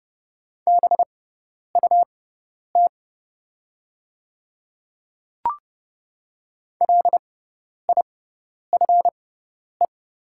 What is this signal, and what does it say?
Morse code, heard as a keyed tone, about 30 wpm, spaced as for 10 wpm, 700 Hz. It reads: BUT LIFE
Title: 6x Character Spacing - Top 300 Words - 30wpm